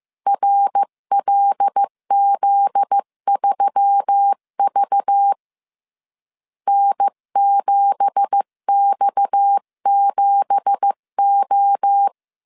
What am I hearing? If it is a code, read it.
Morse code, heard as a keyed tone, about 15 wpm, 800 Hz. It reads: RLZ3V N7X7O
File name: CW / DET / 1 9 1500C S B 14wpm 3302